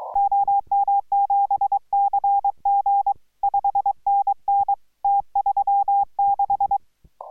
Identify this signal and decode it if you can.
Morse code, heard as a keyed tone, about 23 wpm, 800 Hz. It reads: OM7CG 5NN T36